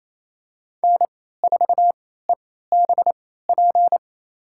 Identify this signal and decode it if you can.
Morse code, heard as a keyed tone, about 28 wpm, spaced as for 16 wpm, 700 Hz. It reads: N4EBP
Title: USA Call Signs - 3x Character Spacing - 28wpm